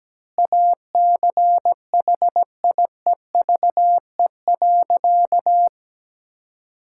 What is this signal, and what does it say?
Morse code, heard as a keyed tone, about 17 wpm, 700 Hz. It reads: ACHIEVE.
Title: Sentences from Top 1000 Words - Review Part 6 - 17wpm